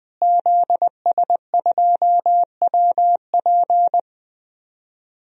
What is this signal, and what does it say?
Morse code, heard as a keyed tone, about 20 wpm, 700 Hz. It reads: ZS2WP
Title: DXCC-100 Call Signs - 5 Characters - Repeated 3x - 20wpm